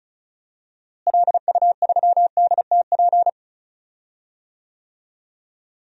Morse code, heard as a keyed tone, about 34 wpm, 700 Hz. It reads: LU3DTP
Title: DXCC-100 Call Signs - 6 Characters - Repeated 3x - 35wpm